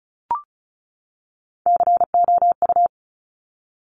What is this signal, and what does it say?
Morse code, heard as a keyed tone, about 36 wpm, 700 Hz. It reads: COU
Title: Top Three Letter Combinations in Words 35wpm